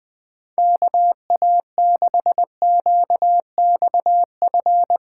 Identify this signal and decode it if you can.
Morse code, heard as a keyed tone, about 20 wpm, 700 Hz. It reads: KA6QXF
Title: USA Callsign DE Callsign - 6 Characters - 20wpm